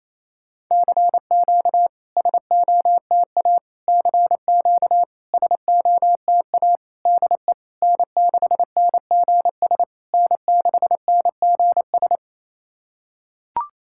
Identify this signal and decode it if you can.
Morse code, heard as a keyed tone, about 25 wpm, 700 Hz. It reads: CQ SOTA CQ SOTA DE N6NGS N6NGS
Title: SOTA QSO Practice 28wpm